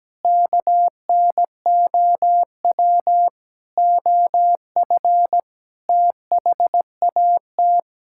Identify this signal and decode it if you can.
Morse code, heard as a keyed tone, about 17 wpm, 700 Hz. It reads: KNOW OF THAT
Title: Sentences from Top 1000 Words - Review Part 5 - 17wpm